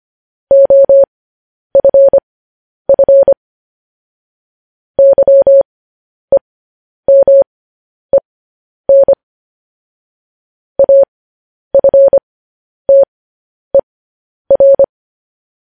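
Morse code, heard as a keyed tone, about 25 wpm, spaced as for 10 wpm, 550 Hz. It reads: OFF YEMEN AFTER
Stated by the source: Morse code sending of news headlines, speed 10.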